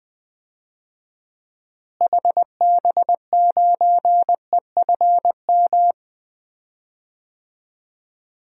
Morse code, heard as a keyed tone, about 21 wpm, 700 Hz. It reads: HB9EFM
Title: DXCC-100 Call Signs - 6 Characters - Repeated 2x - 20wpm